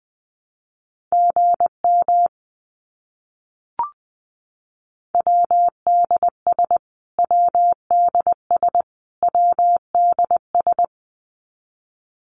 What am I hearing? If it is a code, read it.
Morse code, heard as a keyed tone, about 20 wpm, 700 Hz. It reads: GM WDS WDS WDS
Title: Abbreviations Repeated 3x - 20wpm